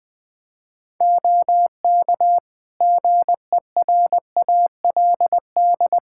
Morse code, heard as a keyed tone, about 20 wpm, 700 Hz. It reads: OK GERALD
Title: Practice QSO - Exchange 3 - More Info - 20wpm